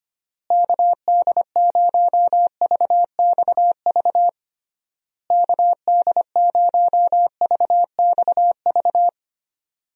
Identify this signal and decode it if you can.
Morse code, heard as a keyed tone, about 25 wpm, 700 Hz. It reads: KD0VXV KD0VXV
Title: USA Call Signs - 6 Characters - Repeated 3x - 25wpm